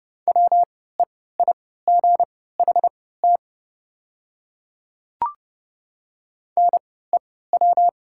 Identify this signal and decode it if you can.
Morse code, heard as a keyed tone, about 30 wpm, spaced as for 17 wpm, 700 Hz. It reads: WEIGHT NEW
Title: 3x Character Spacing - Top 500 Words - 30wpm